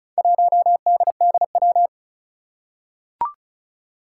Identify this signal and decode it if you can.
Morse code, heard as a keyed tone, about 36 wpm, 700 Hz. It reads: 1DDW
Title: Four Challenging Letter-Number Combinations 35wpm